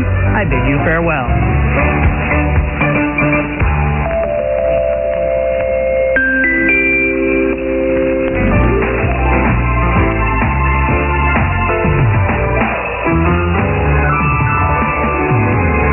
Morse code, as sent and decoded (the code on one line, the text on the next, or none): none